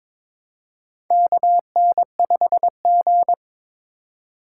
Morse code, heard as a keyed tone, about 22 wpm, 700 Hz. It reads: KN5G
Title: USA Call Signs - 4 Characters - 3x Character Spacing - 22wpm